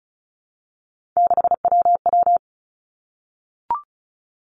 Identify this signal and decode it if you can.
Morse code, heard as a keyed tone, about 36 wpm, 700 Hz. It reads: 6WW
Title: Three Challenging Letter-Number Combinations 35wpm